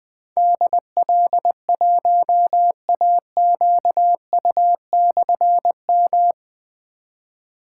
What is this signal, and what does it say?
Morse code, heard as a keyed tone, about 20 wpm, 700 Hz. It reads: DL1AQU/M
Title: Difficult Call Signs 20wpm